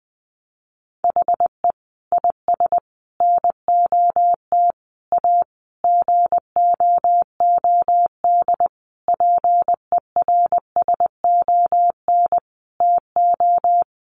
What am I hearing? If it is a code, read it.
Morse code, heard as a keyed tone, about 20 wpm, 700 Hz. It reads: HE IS NOT A GOOD PERSON TO
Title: Sentences from Top 1400 Words 20wpm